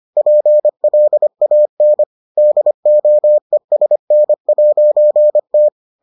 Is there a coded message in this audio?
Yes: Morse code from 0.2 to 5.7 s, heard as a keyed tone, about 25 wpm, 600 Hz. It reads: PLAN DOESN'T